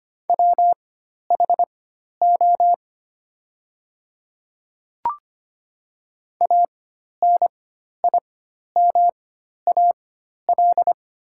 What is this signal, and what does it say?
Morse code, heard as a keyed tone, about 25 wpm, spaced as for 12 wpm, 700 Hz. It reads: WHO ANIMAL
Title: 4x Character Spacing - Top 200 Words - 25wpm